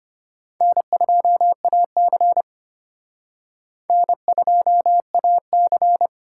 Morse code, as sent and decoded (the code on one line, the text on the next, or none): -. ..--- .- -.-. / -. ..--- .- -.-.
N2AC N2AC